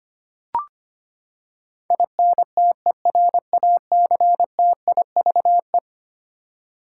Morse code, heard as a keyed tone, about 25 wpm, 700 Hz. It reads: INTERACTIVE
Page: morse-code-ninja-practice-25wpm